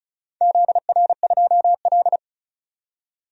Morse code, heard as a keyed tone, about 36 wpm, 700 Hz. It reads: ZR2L
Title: DXCC-100 Call Signs - 4 Characters - 35wpm